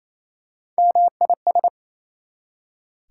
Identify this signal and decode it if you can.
Morse code, heard as a keyed tone, about 28 wpm, 700 Hz. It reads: MIS
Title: Prefixes and Suffixes 28wpm